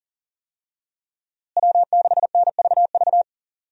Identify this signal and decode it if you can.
Morse code, heard as a keyed tone, about 39 wpm, 700 Hz. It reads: W6NVV
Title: USA Call Signs - 5 Characters - Repeated 3x - 40wpm